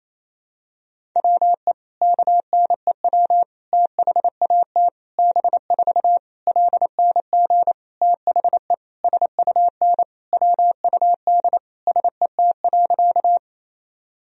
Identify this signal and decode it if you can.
Morse code, heard as a keyed tone, about 27 wpm, 700 Hz. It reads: WE KNEW THAT B4 LNG THE SUN WUD SET.
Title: Sentences from Top 400 Words and Abbreviations 28wpm